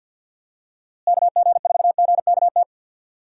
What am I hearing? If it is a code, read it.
Morse code, heard as a keyed tone, about 49 wpm, 700 Hz. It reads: KG4GKT
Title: USA Call Signs - 6 Characters - Repeated 2x - 50wpm